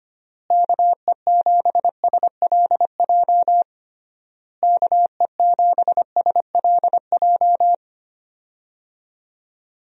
Morse code, heard as a keyed tone, about 25 wpm, 700 Hz. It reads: KE7SLJ KE7SLJ
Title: USA Call Signs - 6 Characters - Repeated 3x - 25wpm